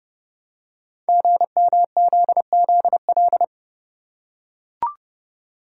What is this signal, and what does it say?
Morse code, heard as a keyed tone, about 28 wpm, 700 Hz. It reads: GMZZL
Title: Code Groups - 5 Letters - 30wpm